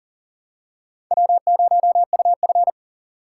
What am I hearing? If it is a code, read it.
Morse code, heard as a keyed tone, about 37 wpm, 700 Hz. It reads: W0UF
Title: K1USN SST QSO Practice - 40wpm